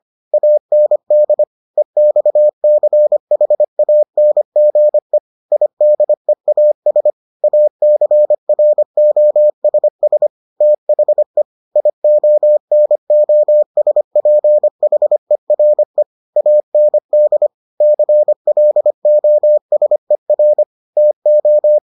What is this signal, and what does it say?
Morse code, heard as a keyed tone, about 25 wpm, 600 Hz. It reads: AND EXCHANGE IDEAS ACROSS THE IONOSPHERE AND CLOSER TO